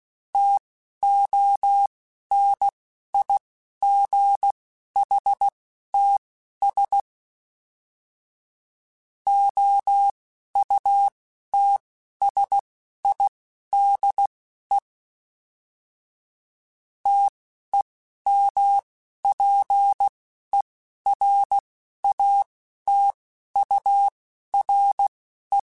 Morse code, heard as a keyed tone, about 16 wpm, spaced as for 11 wpm, 800 Hz. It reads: TONIGHTS OUTSIDE TEMPERATURE
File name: Weather Sets / Weather Sets-mp3s / Weather 5 5 @10